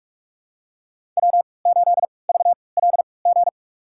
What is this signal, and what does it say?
Morse code, heard as a keyed tone, about 46 wpm, spaced as for 25 wpm, 700 Hz. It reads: W8VLG
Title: USA Call Signs - 3x Character Spacing - 45wpm